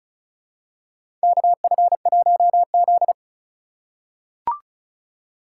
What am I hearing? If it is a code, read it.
Morse code, heard as a keyed tone, about 34 wpm, 700 Hz. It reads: KF1Z
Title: USA Call Signs - 4x Character Spacing - 35wpm